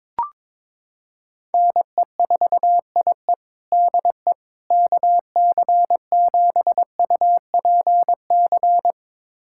Morse code, heard as a keyed tone, about 22 wpm, 700 Hz. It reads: NE4IE DE KC7UPC
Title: USA Callsign DE Callsign 22wpm